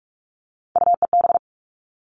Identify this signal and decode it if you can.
Morse code, heard as a keyed tone, about 44 wpm, 700 Hz. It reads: UEB
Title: All Cut Numbers - 3-Digit 45wpm